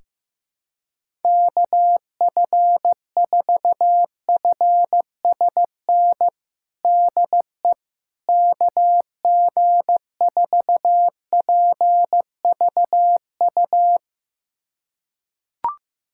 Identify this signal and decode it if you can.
Morse code, heard as a keyed tone, about 15 wpm, 700 Hz. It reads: KF4FSN DE KG4PVU
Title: USA Callsign DE Callsign - 6 Characters - 15wpm